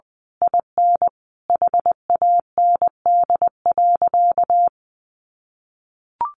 Morse code, heard as a keyed tone, about 20 wpm, 700 Hz. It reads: IN HAND.
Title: Sentences from Top 1200 Words 20wpm